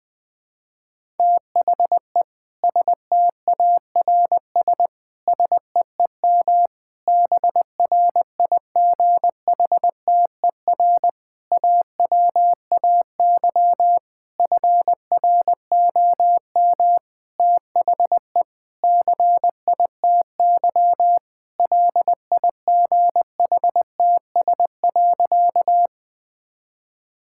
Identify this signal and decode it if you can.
Morse code, heard as a keyed tone, about 20 wpm, 700 Hz. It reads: THE STARS SEEM BRIGHTER AWAY FROM THE CITY LIGHTS.
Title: Difficult Sentences with Hints - 20wpm